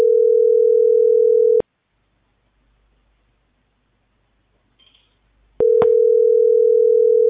Ringback tone (North America), 440 + 480 Hz.